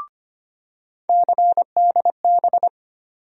Morse code, heard as a keyed tone, about 25 wpm, 700 Hz. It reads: CDB
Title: USA Call Sign Postfixes - 3 Letters 25wpm